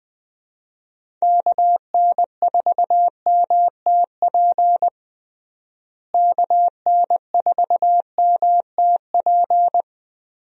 Morse code, heard as a keyed tone, about 20 wpm, 700 Hz. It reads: KN4MTP KN4MTP
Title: USA Call Signs - 6 Characters - Repeated 3x - 20wpm